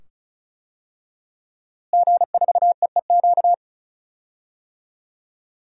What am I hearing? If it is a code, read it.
Morse code, heard as a keyed tone, about 36 wpm, 700 Hz. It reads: G4EEQ
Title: DXCC-100 Call Signs - 5 Characters - Repeated 2x - 35wpm